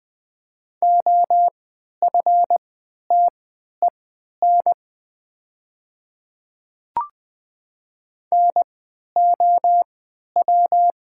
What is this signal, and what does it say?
Morse code, heard as a keyed tone, about 20 wpm, spaced as for 11 wpm, 700 Hz. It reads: OFTEN NOW
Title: 3x Character Spacing - Top 300 Words - 20wpm